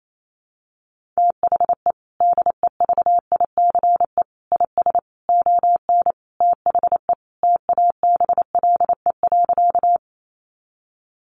Morse code, heard as a keyed tone, about 28 wpm, 700 Hz. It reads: THE DEVICE IS ON THE TABLE.
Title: Sentences from Top 1500 Words 28wpm